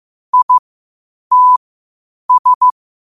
Morse code, heard as a keyed tone, about 15 wpm, spaced as for 8 wpm, 1000 Hz. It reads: ITS
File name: qotdfiles / 8wpm